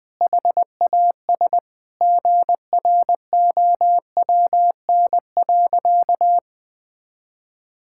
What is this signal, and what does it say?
Morse code, heard as a keyed tone, about 20 wpm, 700 Hz. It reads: HAS GROWN.
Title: Sentences with 4-Word Phrases - Spoken Beginning - 20wpm